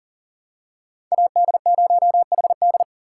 Morse code, heard as a keyed tone, about 40 wpm, 700 Hz. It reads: AD0HD
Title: USA Call Signs - 5 Characters - 4x Character Spacing - 40wpm